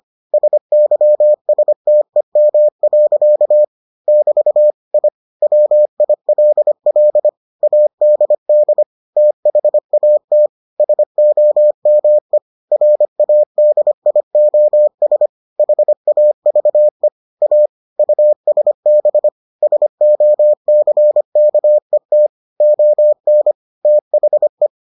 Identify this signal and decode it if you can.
Morse code, heard as a keyed tone, about 25 wpm, 600 Hz. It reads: SYSTEM. <BT> I WILL ADD THAT SOME RADIOS HAVE A USB SOCKET ON THE